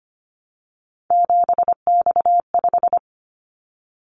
Morse code, heard as a keyed tone, about 25 wpm, 700 Hz. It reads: 7X5
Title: Callsign Prefixes for Top 100 DXCC - 25wpm